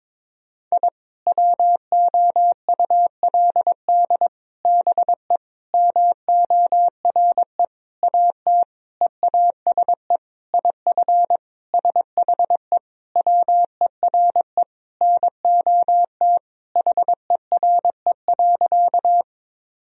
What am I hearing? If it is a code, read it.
Morse code, heard as a keyed tone, about 22 wpm, 700 Hz. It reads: I WOULD BE MORE AT EASE IF SHE WERE NOT HERE.
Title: Sentences from Top 1000 Words - Review Part 1 - 22wpm